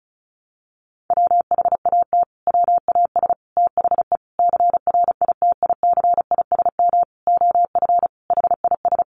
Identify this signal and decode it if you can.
Morse code, heard as a keyed tone, about 34 wpm, 700 Hz. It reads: WHAT WAS THE CRITICISM OF HIS
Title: Sentences from Top 1300 Words 35wpm